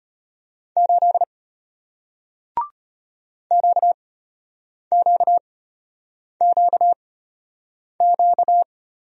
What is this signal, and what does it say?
Morse code, heard as a keyed tone, about 33 wpm, spaced as for 8 wpm, 700 Hz. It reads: 8 QQQQ